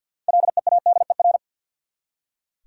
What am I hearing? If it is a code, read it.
Morse code, heard as a keyed tone, about 46 wpm, 700 Hz. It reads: LEADER
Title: Six Letter Words - 50wpm